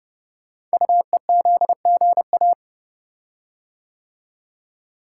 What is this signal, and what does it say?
Morse code, heard as a keyed tone, about 30 wpm, 700 Hz. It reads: UEZGA
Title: Code Groups - 5 Letter:Numbers - 30wpm